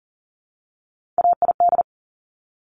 Morse code, heard as a keyed tone, about 42 wpm, 700 Hz. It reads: AID
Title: Top Three Letter Combinations in Words 40wpm